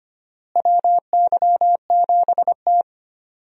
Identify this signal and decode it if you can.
Morse code, heard as a keyed tone, about 25 wpm, 700 Hz. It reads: WY7T
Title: USA Call Signs - 4 Characters - Repeated 2x - 25wpm